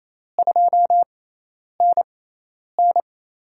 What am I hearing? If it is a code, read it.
Morse code, heard as a keyed tone, about 28 wpm, spaced as for 10 wpm, 700 Hz. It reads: 2NN
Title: USA Call Signs - 4 Characters - 6x Character Spacing - 28wpm